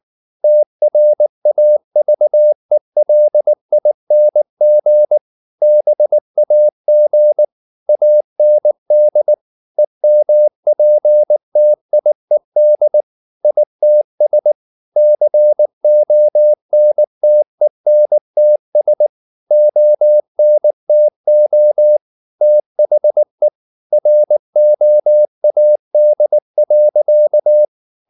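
Morse code, heard as a keyed tone, about 19 wpm, 600 Hz. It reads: TRAVELING BAG AND EMPTIED ITS CONTENTS ONTO THE ROAD.
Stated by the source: Morse code at 19 wpm from CW med Gnister, reading Ken Follett's novel Never in English.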